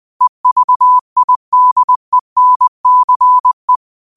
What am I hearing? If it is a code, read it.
Morse code, heard as a keyed tone, about 20 wpm, 1000 Hz. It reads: EVIDENCE
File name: qotdfiles / 20wpm